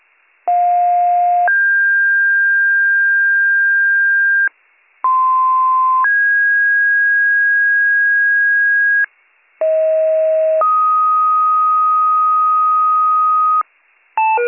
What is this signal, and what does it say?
Two-tone page, heard as a one-second tone, then a three-second tone, about 710 then 1600 Hz, about 1040 then 1650 Hz, about 640 then 1250 Hz. It is heard from 0.5 to 13.6 s.